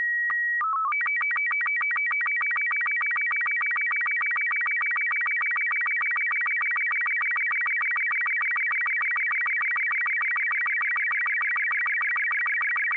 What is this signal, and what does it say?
SSTV (slow-scan television), Robot 36.
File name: music / example / sstv robot36